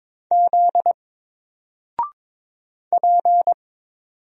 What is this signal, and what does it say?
Morse code, heard as a keyed tone, about 22 wpm, 700 Hz. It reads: Z P
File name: Single-Letter-Number-22wpm